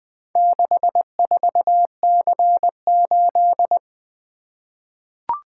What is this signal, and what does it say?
Morse code, heard as a keyed tone, about 20 wpm, 700 Hz. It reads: 64C8
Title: Four Letter-Number Combinations - 20wpm